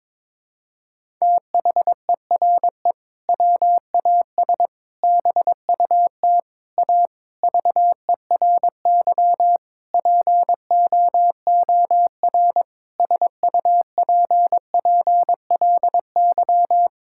Morse code, heard as a keyed tone, about 22 wpm, 700 Hz. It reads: THERE WAS BUT A VERY POOR SUPPLY
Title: Sentences from Top 1000 Words - Review Part 5 - 22wpm